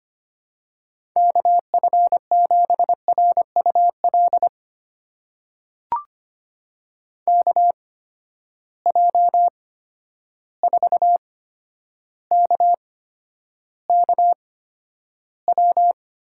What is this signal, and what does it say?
Morse code, heard as a keyed tone, about 25 wpm, spaced as for 11 wpm, 700 Hz. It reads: KF7RUL KJ4KKW
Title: USA Call Signs - 6 Characters - 8x Character Spacing - 25wpm